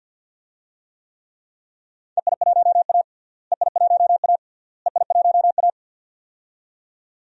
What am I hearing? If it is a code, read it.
Morse code, heard as a keyed tone, about 48 wpm, 700 Hz. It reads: EI1A EI1A EI1A
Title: DXCC-100 Call Signs - 4 Characters - Repeated 3x - 50wpm